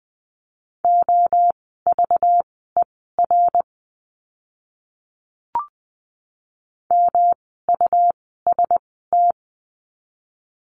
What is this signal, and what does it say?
Morse code, heard as a keyed tone, about 20 wpm, spaced as for 14 wpm, 700 Hz. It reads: OVER MUST
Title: 237 Lesson 38f - Letter q - Words 20wpm